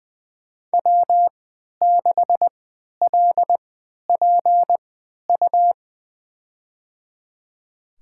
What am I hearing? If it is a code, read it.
Morse code, heard as a keyed tone, about 20 wpm, spaced as for 11 wpm, 700 Hz. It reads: W6LPU
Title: USA Call Signs - 5 Characters - 3x Character Spacing - 20wpm